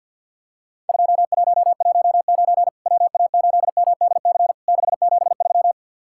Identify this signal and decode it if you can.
Morse code, heard as a keyed tone, about 52 wpm, 700 Hz. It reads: 2119 WA8MDC 673
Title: Worked All Europe DX Contest - QTC Exchange - 50wpm